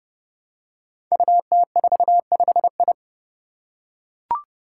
Morse code, heard as a keyed tone, about 32 wpm, 700 Hz. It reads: UT45I
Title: Five Challenging Letter-Number Combinations 30wpm